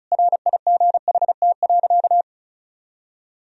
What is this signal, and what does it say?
Morse code, heard as a keyed tone, about 36 wpm, 700 Hz. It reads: RIGHT.
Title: Sentences from Top 1000 Words - Review Part 4 - 35wpm